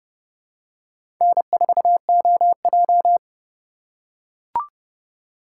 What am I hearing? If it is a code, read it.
Morse code, heard as a keyed tone, about 31 wpm, 700 Hz. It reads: N4OJ
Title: USA Call Signs - 4 Characters - 2x Character Spacing - 30wpm